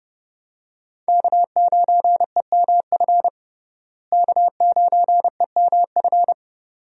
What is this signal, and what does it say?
Morse code, heard as a keyed tone, about 31 wpm, 700 Hz. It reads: K9EMF K9EMF
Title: USA Call Signs - 5 Characters - Repeated 2x - 30wpm